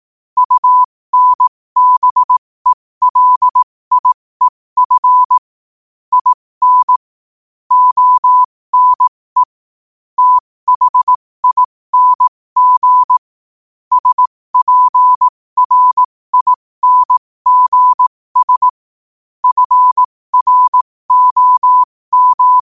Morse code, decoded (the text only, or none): UNBELIEF IN ONE THING SPRINGS FROM